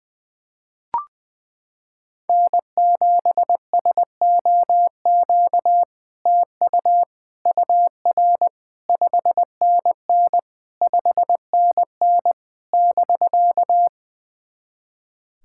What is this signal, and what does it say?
Morse code, heard as a keyed tone, about 21 wpm, 700 Hz. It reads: N7SOQ TU UR 5NN 5NN <BK>
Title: SOTA QSO Practice 20wpm